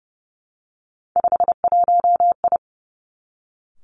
Morse code, heard as a keyed tone, about 30 wpm, 700 Hz. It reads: 51I
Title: Three Challenging Letter-Number Combinations 30wpm